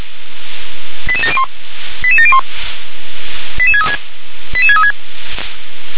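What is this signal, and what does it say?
Five-tone selective call (ZVEI), heard as quick runs of five tones, tones of about 70 ms. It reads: 88701 88701 88636 88636
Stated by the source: ZVEI five-tone selective call received on SDR.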